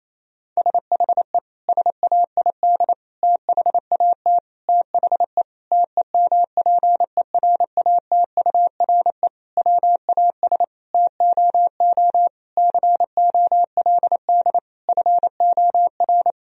Morse code, heard as a keyed tone, about 28 wpm, 700 Hz. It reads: SHE SAID THAT THE TEMPERATURE WAS TOO COLD FOR
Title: Sentences from Top 1000 Words - Review Part 4 - 28wpm